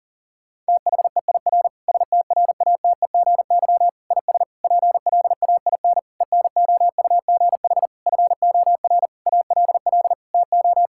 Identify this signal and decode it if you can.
Morse code, heard as a keyed tone, about 37 wpm, 700 Hz. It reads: THEIR STRATEGY IS PLAIN ENOUGH FOR ALL TO
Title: Sentences from Top 1000 Words - Review Part 5 - 40wpm